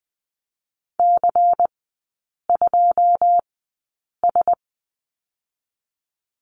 Morse code, heard as a keyed tone, about 20 wpm, spaced as for 8 wpm, 700 Hz. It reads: C2S
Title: 127 Lesson 21e - Number 6 - Three Characters 20wpm